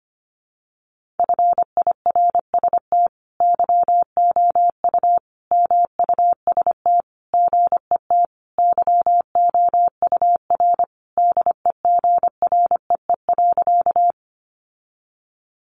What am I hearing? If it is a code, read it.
Morse code, heard as a keyed tone, about 25 wpm, 700 Hz. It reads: FIRST YOU MUST GET YOUR DEGREE.
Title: Sentences from Top 1500 Words 25wpm